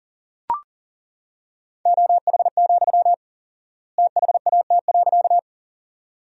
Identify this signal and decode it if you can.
Morse code, heard as a keyed tone, about 40 wpm, 700 Hz. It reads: OH, THAT.
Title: 4x Word-Spacing - Sentences from Top 500 Words - 40wpm